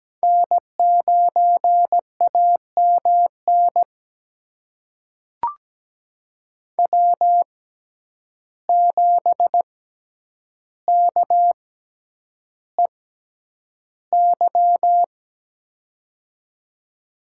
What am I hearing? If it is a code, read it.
Morse code, heard as a keyed tone, about 17 wpm, spaced as for 9 wpm, 700 Hz. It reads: N9AMN W7KEY
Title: USA Call Signs - 5 Characters - 6x Character Spacing - 17wpm